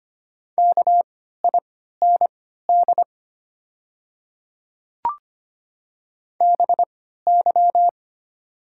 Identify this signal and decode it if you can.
Morse code, heard as a keyed tone, about 25 wpm, spaced as for 14 wpm, 700 Hz. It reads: KIND BY
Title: 3x Character Spacing - Top 300 Words - 25wpm